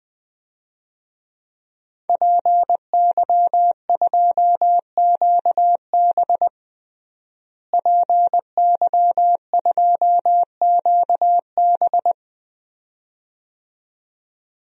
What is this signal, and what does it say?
Morse code, heard as a keyed tone, about 20 wpm, 700 Hz. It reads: PY2QB PY2QB
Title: DXCC-100 Call Signs - 5 Characters - Repeated 2x - 20wpm